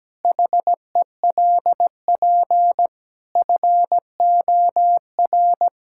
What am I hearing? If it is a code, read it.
Morse code, heard as a keyed tone, about 17 wpm, 700 Hz. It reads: HELP FOR
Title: Pangram Challenge v2 - 17wpm